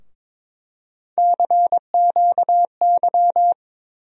Morse code, heard as a keyed tone, about 22 wpm, 700 Hz. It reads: CQY
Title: USA Call Sign Postfixes - 3 Letters 22wpm